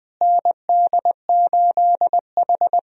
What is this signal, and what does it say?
Morse code, heard as a keyed tone, about 20 wpm, 700 Hz. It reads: ND8H
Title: USA Call Signs - 4 Characters - Repeated 3x - 20wpm